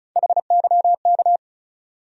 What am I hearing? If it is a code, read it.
Morse code, heard as a keyed tone, about 34 wpm, 700 Hz. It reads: HYK
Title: USA Call Sign Postfixes - 3 Letters 35wpm